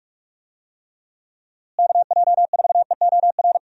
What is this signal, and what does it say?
Morse code, heard as a keyed tone, about 48 wpm, 700 Hz. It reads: KJ4EOR
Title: USA Call Signs - 6 Characters - Repeated 2x - 45wpm